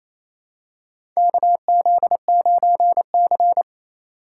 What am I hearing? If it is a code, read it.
Morse code, heard as a keyed tone, about 28 wpm, 700 Hz. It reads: KZ9C
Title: USA Call Signs - 4 Characters - 2x Character Spacing - 28wpm